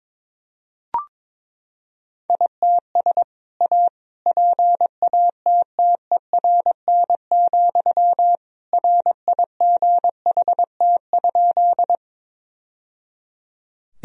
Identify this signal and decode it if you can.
Morse code, heard as a keyed tone, about 22 wpm, 700 Hz. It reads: ITS A PATTERN, RIGHT?
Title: Sentences from Top 1000 Words - Review Part 2 - 22wpm